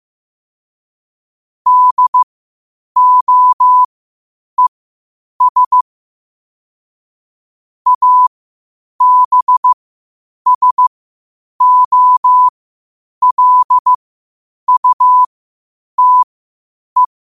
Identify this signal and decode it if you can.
Morse code, heard as a keyed tone, about 15 wpm, spaced as for 8 wpm, 1000 Hz. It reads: DOES ABSOLUTE
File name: qotdfiles / 8wpm